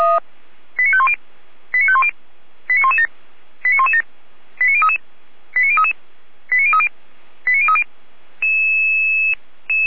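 Five-tone selective call (ZVEI), heard as quick runs of five tones, tones of about 70 ms. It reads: 89510 89510 89117 89117 89033 89033 89030 89030
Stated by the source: ZVEI five-tone selective call received by SDR.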